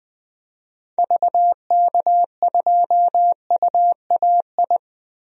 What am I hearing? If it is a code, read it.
Morse code, heard as a keyed tone, about 20 wpm, 700 Hz. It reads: VK2UAI
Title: DXCC-100 Call Signs - 6 Characters - 20wpm